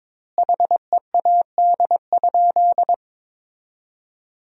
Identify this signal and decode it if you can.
Morse code, heard as a keyed tone, about 22 wpm, 700 Hz. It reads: HEAD?
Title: 2x Word-Spacing - Sentences from Top 300 Words - 22wpm